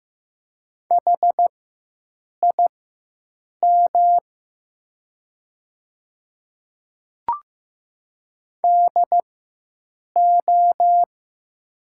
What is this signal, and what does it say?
Morse code, heard as a keyed tone, about 15 wpm, spaced as for 7 wpm, 700 Hz. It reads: HIM DO
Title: 4x Character Spacing - Top 100 Words - 15wpm